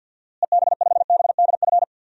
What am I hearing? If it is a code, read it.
Morse code, heard as a keyed tone, about 53 wpm, 700 Hz. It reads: EB5BDF
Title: DXCC-100 Call Signs - 6 Characters - Repeated 3x - 50wpm